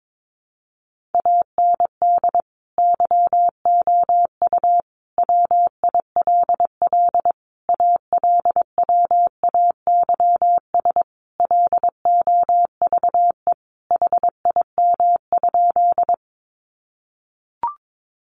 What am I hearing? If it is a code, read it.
Morse code, heard as a keyed tone, about 22 wpm, 700 Hz. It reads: AND YOU WILL ALWAYS LOVE HIM?
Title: Sentences from Top 400 Words 22wpm